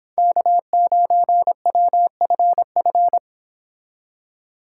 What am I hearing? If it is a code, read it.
Morse code, heard as a keyed tone, about 29 wpm, 700 Hz. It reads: K9WFF